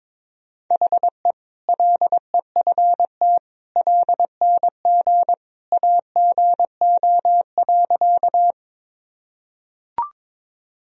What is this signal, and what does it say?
Morse code, heard as a keyed tone, about 22 wpm, 700 Hz. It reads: HE LEFT LNG AGO.
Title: Sentences from Top 500 Words and Abbreviations 22wpm